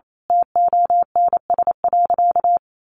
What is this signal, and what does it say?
Morse code, heard as a keyed tone, about 28 wpm, 700 Hz. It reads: TONS.
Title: A Sound Of Thunder 28wpm